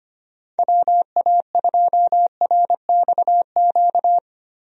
Morse code, heard as a keyed tone, about 25 wpm, 700 Hz. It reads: WA2RXQ
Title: USA Call Signs - 6 Characters - Repeated 2x - 25wpm